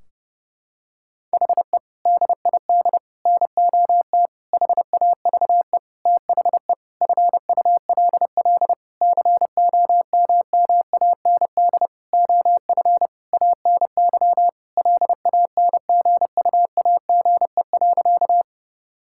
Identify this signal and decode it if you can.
Morse code, heard as a keyed tone, about 30 wpm, 700 Hz. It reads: HE DID NOT HAVE THE FULL COMMAND OF ANY LANGUAGE.
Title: Sentences from Top 1600 Words 30wpm